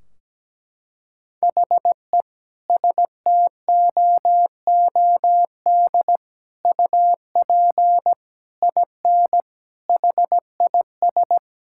Morse code, heard as a keyed tone, about 17 wpm, 700 Hz. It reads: HE STOOD UP IN HIS
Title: Sentences from Top 1600 Words 17wpm